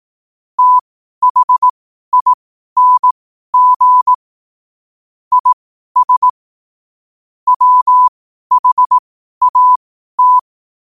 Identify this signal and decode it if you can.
Morse code, heard as a keyed tone, about 18 wpm, spaced as for 12 wpm, 1000 Hz. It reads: THING IS WHAT
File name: qotdfiles / 12wpm